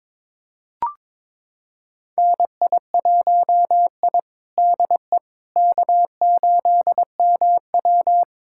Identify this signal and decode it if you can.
Morse code, heard as a keyed tone, about 22 wpm, 700 Hz. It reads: NI1I DE K8MW
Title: USA Callsign DE Callsign - 4 Characters - 22wpm